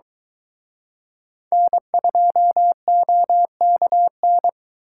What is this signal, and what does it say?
Morse code, heard as a keyed tone, about 23 wpm, 700 Hz. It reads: N2OKN